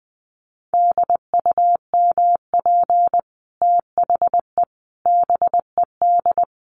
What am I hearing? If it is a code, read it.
Morse code, heard as a keyed tone, about 20 wpm, 700 Hz. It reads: DUMP THE BED
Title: Sentences from Top 1300 Words 20wpm